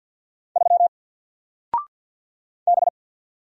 Morse code, heard as a keyed tone, about 53 wpm, 700 Hz. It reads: 3 B
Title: Single Letter-Number 50wpm